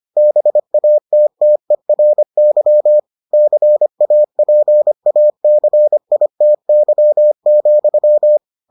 Morse code, heard as a keyed tone, about 25 wpm, 600 Hz. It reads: BATTERY CAPACITY,